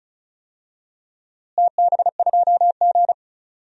Morse code, heard as a keyed tone, about 34 wpm, 700 Hz. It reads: TB2G
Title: DXCC-100 Call Signs - 4 Characters - Repeated 3x - 35wpm